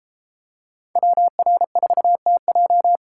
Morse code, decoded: WR4TJ

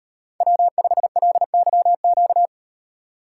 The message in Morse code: W5LYQ

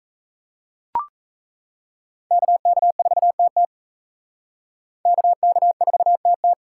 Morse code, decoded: KK4TT KK4TT

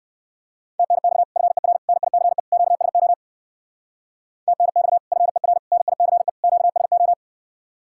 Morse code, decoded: TNX FER NICE QSO TNX FER NICE QSO

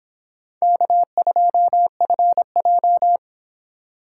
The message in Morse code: K2FJ

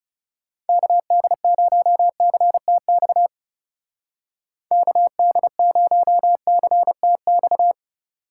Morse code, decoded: KD0CTX KD0CTX

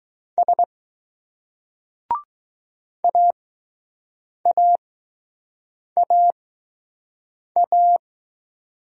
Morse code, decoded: S AAAA

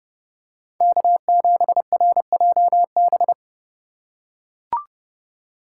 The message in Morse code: K7RJB

